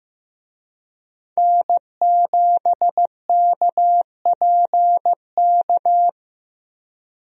Morse code, heard as a keyed tone, about 15 wpm, 700 Hz. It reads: N7KPK
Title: USA Call Signs - 5 Characters - Repeated 2x - 15wpm